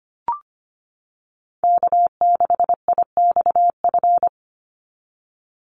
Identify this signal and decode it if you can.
Morse code, heard as a keyed tone, about 25 wpm, 700 Hz. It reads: K6IXF
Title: USA Call Signs 25wpm